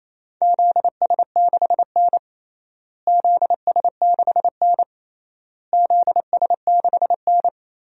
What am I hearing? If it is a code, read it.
Morse code, heard as a keyed tone, about 28 wpm, 700 Hz. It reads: ZS6N ZS6N ZS6N
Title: DXCC-100 Call Signs - 4 Characters - Repeated 3x - 28wpm